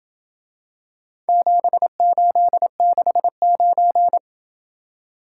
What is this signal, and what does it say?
Morse code, heard as a keyed tone, about 27 wpm, 700 Hz. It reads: 7869